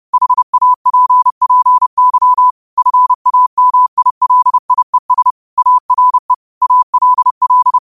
Morse code, heard as a keyed tone, about 30 wpm, 1000 Hz. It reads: HAPPY FAMILIES ARE ALL
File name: qotdfiles / 30wpm